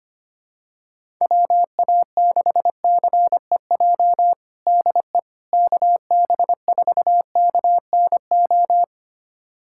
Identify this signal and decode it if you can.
Morse code, heard as a keyed tone, about 25 wpm, 700 Hz. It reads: WA6CEJ DE KB4KNO
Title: USA Callsign DE Callsign - 6 Characters - 25wpm